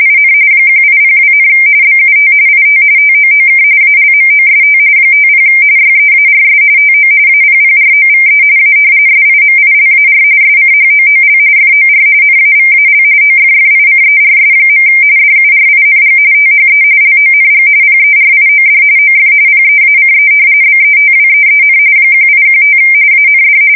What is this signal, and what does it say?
RTTY, about 45.45 baud. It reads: IN 1802, BUT IT TOOK ALMOST 50 YEARS TO APPROVE A SIMILAR SCHOOL FOR NAVAL OFFICERS. DURING THE LONG PERIOD OF PEACE BETWEEN 1815 AND
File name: fsk170 / usnavy2